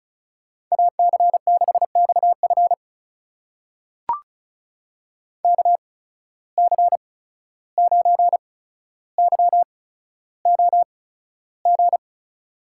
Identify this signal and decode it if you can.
Morse code, heard as a keyed tone, about 36 wpm, spaced as for 14 wpm, 700 Hz. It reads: AC6XF KC9YOG
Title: USA Call Signs - 8x Character Spacing - 35wpm